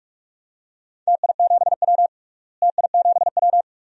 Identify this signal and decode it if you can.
Morse code, heard as a keyed tone, about 48 wpm, 690 Hz. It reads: TI7W TI7W
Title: DXCC-100 Call Signs - 4 Characters - Repeated 3x - 45wpm